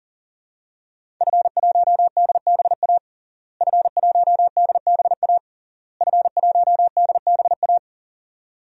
Morse code, heard as a keyed tone, about 40 wpm, 700 Hz. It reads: F1DBA F1DBA F1DBA
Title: DXCC-100 Call Signs - 5 Characters - Repeated 3x - 40wpm